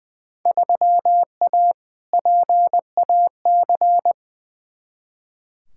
Morse code, heard as a keyed tone, about 20 wpm, 700 Hz. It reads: 3A PAC
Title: Field Day Exchanges Encore - 20wpm